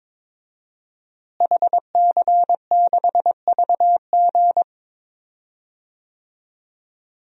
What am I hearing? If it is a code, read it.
Morse code, heard as a keyed tone, about 22 wpm, 700 Hz. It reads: HC6VG
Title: DXCC-100 Call Signs - 5 Characters - Repeated 3x - 22wpm